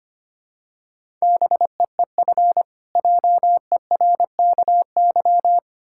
Morse code, heard as a keyed tone, about 25 wpm, 700 Hz. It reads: BEEF JERKY
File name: practice / 25wpm / Difficult Words with Hints - 25wpm